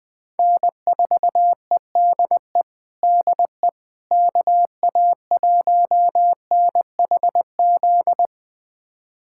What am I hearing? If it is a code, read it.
Morse code, heard as a keyed tone, about 21 wpm, 700 Hz. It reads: N4EDE DE KA1NHZ
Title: USA Callsign DE Callsign 20wpm